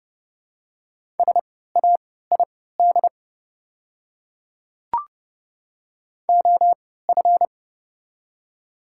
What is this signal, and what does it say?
Morse code, heard as a keyed tone, about 30 wpm, spaced as for 17 wpm, 700 Hz. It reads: SAID OF
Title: 3x Character Spacing - Top 100 Words - 30wpm